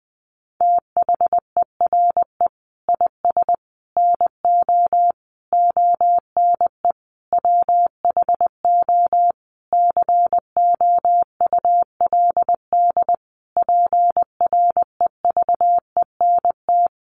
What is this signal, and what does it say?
Morse code, heard as a keyed tone, about 19 wpm, 700 Hz. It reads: THERE IS NO ONE WHO COULD PREVENT
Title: Sentences from Top 1100 Words 20wpm